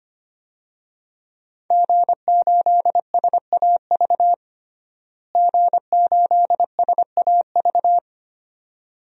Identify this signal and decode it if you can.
Morse code, heard as a keyed tone, about 25 wpm, 700 Hz. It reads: G8SAV G8SAV
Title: DXCC-100 Call Signs - 5 Characters - Repeated 2x - 25wpm